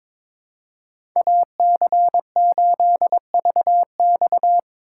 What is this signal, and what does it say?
Morse code, heard as a keyed tone, about 22 wpm, 700 Hz. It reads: AC8VX
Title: USA Call Signs - 3x Character Spacing - 22wpm